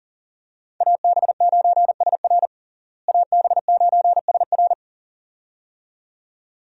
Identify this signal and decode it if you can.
Morse code, heard as a keyed tone, about 37 wpm, 700 Hz. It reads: AB9SR AB9SR
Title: USA Call Signs - 5 Characters - Repeated 2x - 40wpm